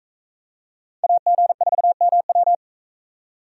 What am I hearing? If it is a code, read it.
Morse code, heard as a keyed tone, about 41 wpm, 700 Hz. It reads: AG4MW